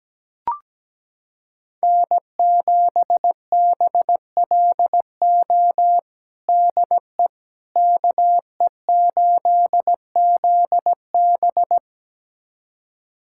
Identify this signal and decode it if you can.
Morse code, heard as a keyed tone, about 17 wpm, 700 Hz. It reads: N7BLO DE KE8ZB
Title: USA Callsign DE Callsign - 5 Characters - 17wpm